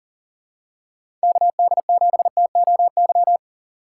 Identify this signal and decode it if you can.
Morse code, heard as a keyed tone, about 40 wpm, 700 Hz. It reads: KD7TOY
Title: USA Call Signs - 6x Character Spacing - 40wpm